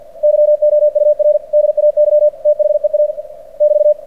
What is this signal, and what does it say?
Morse code, heard as a keyed tone, about 48 wpm, 600 Hz. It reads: YOUR DAY THEA <BT>
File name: archived / CRSnet